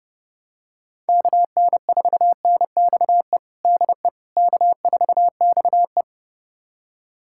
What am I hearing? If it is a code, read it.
Morse code, heard as a keyed tone, about 32 wpm, 700 Hz. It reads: KN4NXE DE K4XE
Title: USA Callsign DE Callsign 30wpm